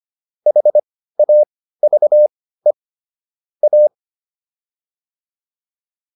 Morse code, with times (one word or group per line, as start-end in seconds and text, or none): 0.5-2.7 HAVE
3.6-3.9 A